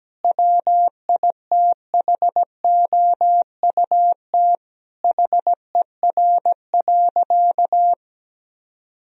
Morse code, heard as a keyed tone, about 17 wpm, 700 Hz. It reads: WITHOUT HER.
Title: Sentences from Top 1000 Words - Review Part 3 - 17wpm